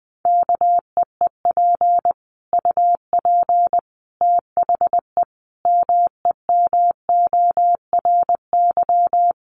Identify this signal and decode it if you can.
Morse code, heard as a keyed tone, about 20 wpm, 700 Hz. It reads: KEEP UP THE MEMORY
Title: Sentences from Top 1300 Words 20wpm